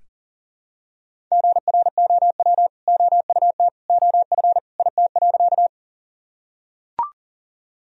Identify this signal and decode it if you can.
Morse code, heard as a keyed tone, about 38 wpm, 700 Hz. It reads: GROW OUT OF IT.
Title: Sentences from Top 1000 Words - Review Part 1 - 40wpm